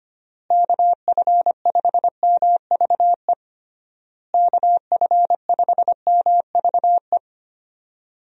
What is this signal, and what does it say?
Morse code, heard as a keyed tone, about 25 wpm, 700 Hz. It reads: KF5MVE KF5MVE
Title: USA Call Signs - 6 Characters - Repeated 3x - 25wpm